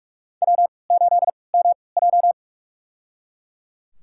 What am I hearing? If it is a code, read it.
Morse code, heard as a keyed tone, about 41 wpm, spaced as for 25 wpm, 700 Hz. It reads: W8MJ
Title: USA Call Signs - 3x Character Spacing - 45wpm